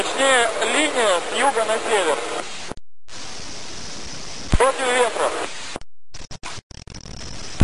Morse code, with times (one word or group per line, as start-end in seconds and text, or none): none